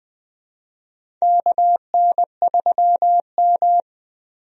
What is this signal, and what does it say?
Morse code, heard as a keyed tone, about 20 wpm, 700 Hz. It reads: KN3M
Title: USA Call Signs - 4 Characters - Repeated 2x - 20wpm